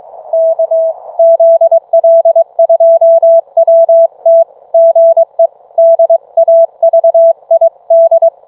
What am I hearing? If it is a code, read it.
Morse code, heard as a keyed tone, about 22 wpm, 660 Hz. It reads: K ZL2WT GE DAVID